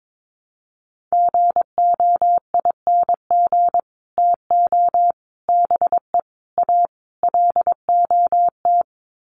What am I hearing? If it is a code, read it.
Morse code, heard as a keyed tone, about 22 wpm, 700 Hz. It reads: GOING TO BE A LOT
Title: Sets of 5 Words Encore 22wpm